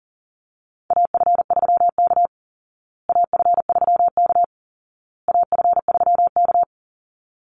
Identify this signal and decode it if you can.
Morse code, heard as a keyed tone, about 40 wpm, 700 Hz. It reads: AF3K AF3K AF3K
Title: Call Signs Repeated 2x - 40wpm